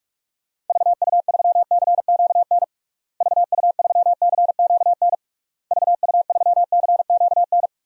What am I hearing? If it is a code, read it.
Morse code, heard as a keyed tone, about 41 wpm, 700 Hz. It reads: VU3CQN VU3CQN VU3CQN
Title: DXCC-100 Call Signs - 6 Characters - Repeated 3x - 45wpm